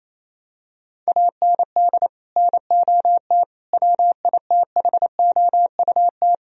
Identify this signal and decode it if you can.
Morse code, heard as a keyed tone, about 28 wpm, 700 Hz. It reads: AND NOT WITHOUT
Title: Sentences from Top 1000 Words - Review Part 3 - 28wpm